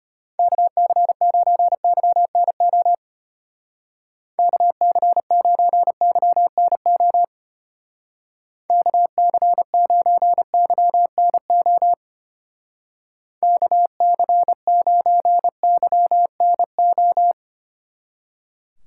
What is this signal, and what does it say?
Morse code, heard as a keyed tone, about 31 wpm, 700 Hz. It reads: KC9YNO KC9YNO KC9YNO KC9YNO
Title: Speed Racing - USA 2x3 Call Signs 25wpm